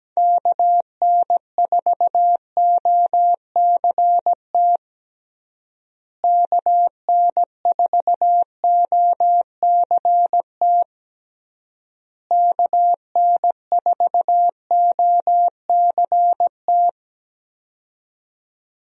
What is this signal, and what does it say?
Morse code, heard as a keyed tone, about 17 wpm, 700 Hz. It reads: KN4OCT KN4OCT KN4OCT
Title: USA Call Signs - 6 Characters - Repeated 3x - 17wpm